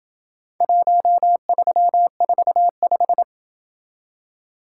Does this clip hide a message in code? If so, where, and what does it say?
Yes: Morse code from 0.6 to 3.2 s, heard as a keyed tone, about 27 wpm, 700 Hz. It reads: 1345